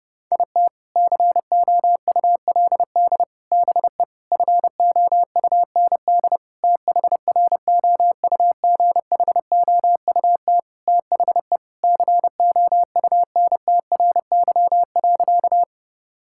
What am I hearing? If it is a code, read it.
Morse code, heard as a keyed tone, about 30 wpm, 700 Hz. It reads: IT COULD BE FOUND THROUGHOUT THE COUNTRY.
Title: Sentences from Top 1000 Words - Review Part 5 - 30wpm